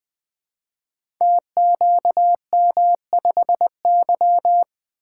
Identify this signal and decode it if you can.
Morse code, heard as a keyed tone, about 21 wpm, 700 Hz. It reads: TQM5Y
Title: Code Groups - 5 Letter:Numbers - 20wpm